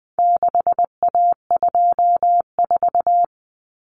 Morse code, heard as a keyed tone, about 20 wpm, 700 Hz. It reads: 6A24
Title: Common Cut Numbers - 4-Digit 20wpm